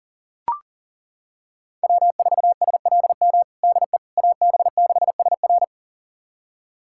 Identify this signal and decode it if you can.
Morse code, heard as a keyed tone, about 40 wpm, 700 Hz. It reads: W4SLM DE AB6SR
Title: USA Callsign DE Callsign - 5 Characters - 40wpm